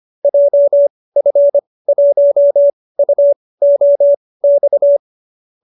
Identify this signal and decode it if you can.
Morse code, heard as a keyed tone, about 25 wpm, spaced as for 18 wpm, 590 Hz. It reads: JF1UOX
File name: qso216b 18